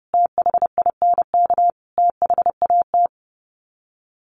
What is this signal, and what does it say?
Morse code, heard as a keyed tone, about 30 wpm, 700 Hz. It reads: THINK THAT
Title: Sets of 4 Words 30wpm